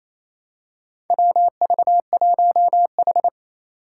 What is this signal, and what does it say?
Morse code, heard as a keyed tone, about 28 wpm, 700 Hz. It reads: WV1H
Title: Four Challenging Letter-Number Combinations 28wpm